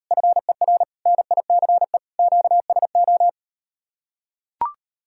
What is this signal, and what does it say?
Morse code, heard as a keyed tone, about 38 wpm, 700 Hz. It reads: FER NICE QSO